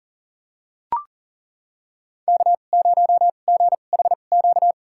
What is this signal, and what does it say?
Morse code, heard as a keyed tone, about 40 wpm, spaced as for 29 wpm, 700 Hz. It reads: K0GHQ
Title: USA Call Signs - 5 Characters - 2x Character Spacing - 40wpm